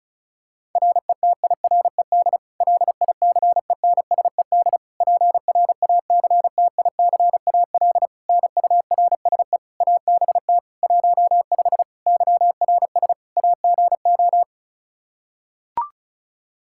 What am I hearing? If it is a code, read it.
Morse code, heard as a keyed tone, about 36 wpm, 700 Hz. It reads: RETIRED LICENSED PRACTICAL NURSE ABT 15 YRS AGO
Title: K1USN SST QSO Practice 35wpm